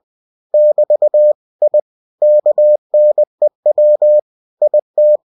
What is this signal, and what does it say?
Morse code, heard as a keyed tone, about 20 wpm, 600 Hz. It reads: <BT> I KNEW IT
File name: cw-books / winnie-the-pooh / winnie-20wpm-page-0074